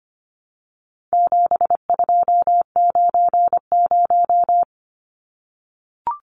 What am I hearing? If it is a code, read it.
Morse code, heard as a keyed tone, about 25 wpm, 700 Hz. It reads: 7290
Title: Numbers - 4-Digit 25wpm